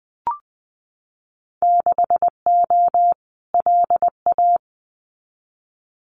Morse code, heard as a keyed tone, about 20 wpm, 700 Hz. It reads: 6O LA
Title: Winter Field Day Exchanges Encore 20wpm